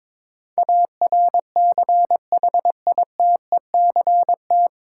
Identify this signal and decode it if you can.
Morse code, heard as a keyed tone, about 22 wpm, 700 Hz. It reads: ARCHITECT
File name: 9 Letter Words - First 3 Letters Given - 22wpm